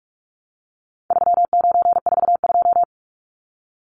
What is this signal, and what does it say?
Morse code, heard as a keyed tone, about 44 wpm, 700 Hz. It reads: 3942
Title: Numbers - 4-Digit 45wpm